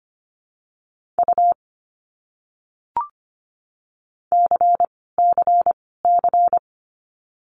Morse code, heard as a keyed tone, about 25 wpm, 700 Hz. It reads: U C C C
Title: Abbreviations Repeated 3x - 25wpm